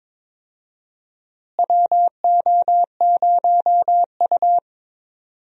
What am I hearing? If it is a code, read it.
Morse code, heard as a keyed tone, about 22 wpm, 700 Hz. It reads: WO0U
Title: USA Call Signs - 4 Characters - Repeated 3x - 22wpm